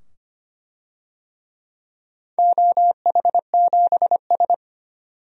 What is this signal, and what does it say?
Morse code, heard as a keyed tone, about 25 wpm, 700 Hz. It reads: OH7S
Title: DXCC-100 Call Signs - 4 Characters - 25wpm